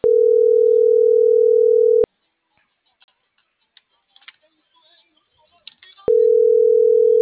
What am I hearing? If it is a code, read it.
Ringback tone (North America), 440 + 480 Hz.